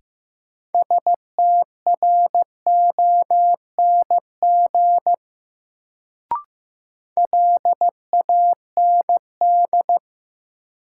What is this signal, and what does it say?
Morse code, heard as a keyed tone, about 15 wpm, 700 Hz. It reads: STRONG LAND
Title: Sentences from Top 500 Words - New Word Drill - 15wpm